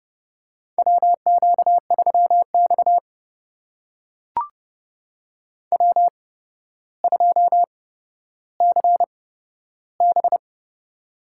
Morse code, heard as a keyed tone, about 30 wpm, spaced as for 13 wpm, 700 Hz. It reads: WQ3X W2CB
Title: USA Call Signs - 4 Characters - 8x Character Spacing - 30wpm